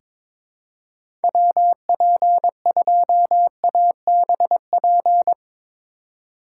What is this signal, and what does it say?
Morse code, heard as a keyed tone, about 22 wpm, 700 Hz. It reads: WP2ABP
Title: USA Call Signs - 6 Characters - 4x Character Spacing - 22wpm